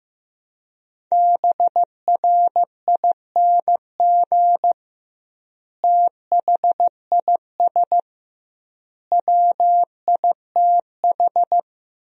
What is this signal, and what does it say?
Morse code, heard as a keyed tone, about 15 wpm, 700 Hz. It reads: BRING THIS WITH
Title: 2x Word-Spacing - Sentences from Top 500 Words - 15wpm